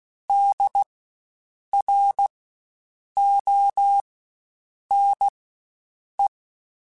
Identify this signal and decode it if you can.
Morse code, heard as a keyed tone, about 16 wpm, spaced as for 7 wpm, 800 Hz. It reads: DRONE
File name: Spaced Alphas / Spaced Alphas mp3s / Ds sp @15